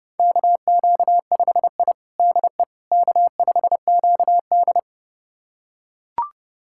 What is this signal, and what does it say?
Morse code, heard as a keyed tone, about 30 wpm, 700 Hz. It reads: KQ5I DE K5QD
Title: USA Callsign DE Callsign - 4 Characters - 30wpm